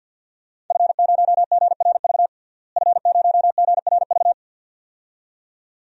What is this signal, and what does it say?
Morse code, heard as a keyed tone, about 51 wpm, 700 Hz. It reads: F0GRV F0GRV
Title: DXCC-100 Call Signs - 5 Characters - Repeated 2x - 50wpm